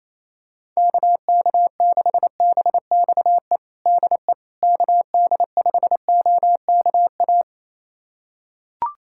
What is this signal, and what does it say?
Morse code, heard as a keyed tone, about 28 wpm, 700 Hz. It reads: KK6BXE DE KD5OKA
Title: USA Callsign DE Callsign - 6 Characters - 28wpm